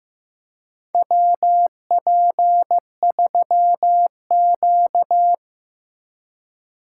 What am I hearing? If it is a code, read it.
Morse code, heard as a keyed tone, about 15 wpm, 700 Hz. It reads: WP3Q
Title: USA Call Signs - 4 Characters - 4x Character Spacing - 15wpm